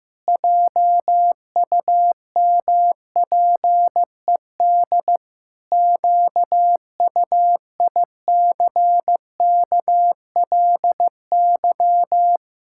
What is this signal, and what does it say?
Morse code, heard as a keyed tone, about 15 wpm, 700 Hz. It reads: JUMPED QUICKLY
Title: Pangram Challenge v2 - 15wpm